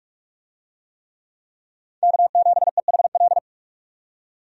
Morse code, heard as a keyed tone, about 46 wpm, 700 Hz. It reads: K7EHL
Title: USA Call Signs - 5 Characters - 45wpm